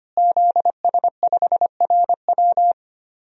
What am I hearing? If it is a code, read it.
Morse code, heard as a keyed tone, about 25 wpm, 700 Hz. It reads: ZS5RW
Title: DXCC-100 Call Signs - 5 Characters - Repeated 3x - 25wpm